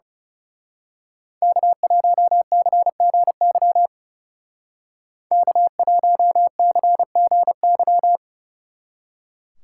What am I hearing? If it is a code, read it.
Morse code, heard as a keyed tone, about 34 wpm, 700 Hz. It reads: K1CGY K1CGY
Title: Speed Racing - USA 1x3 Call Signs 30wpm